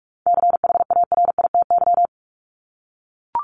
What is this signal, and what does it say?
Morse code, heard as a keyed tone, about 44 wpm, 700 Hz. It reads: CHARITY
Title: Top 1501-1750 Words 45wpm